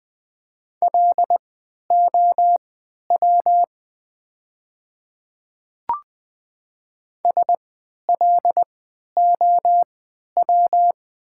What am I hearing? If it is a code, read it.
Morse code, heard as a keyed tone, about 20 wpm, spaced as for 11 wpm, 690 Hz. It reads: LOW SLOW
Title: 3x Character Spacing - Top 500 Words - 20wpm